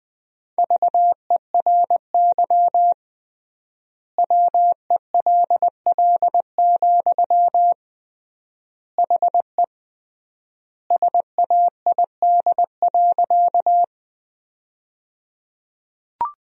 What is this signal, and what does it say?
Morse code, heard as a keyed tone, about 20 wpm, 700 Hz. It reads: VERY WELL, HE SAID.
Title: 3x Word-Spacing - Sentences from Top 200 Words - 20wpm